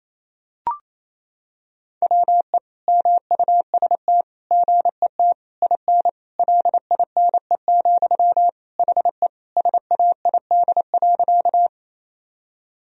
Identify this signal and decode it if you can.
Morse code, heard as a keyed tone, about 28 wpm, 700 Hz. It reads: WE MUST GET IN LINE, HE SAID.
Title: Sentences from Top 1000 Words - Review Part 1 - 28wpm